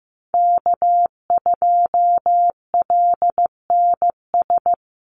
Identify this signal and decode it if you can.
Morse code, heard as a keyed tone, about 15 wpm, 700 Hz. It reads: K2LNS
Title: Call Signs Repeated 3x - 15wpm